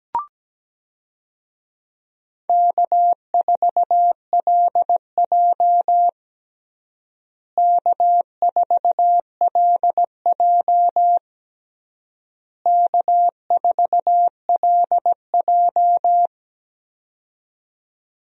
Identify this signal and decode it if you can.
Morse code, heard as a keyed tone, about 17 wpm, 700 Hz. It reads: K4LJ K4LJ K4LJ
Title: USA Call Signs - 4 Characters - Repeated 3x - 17wpm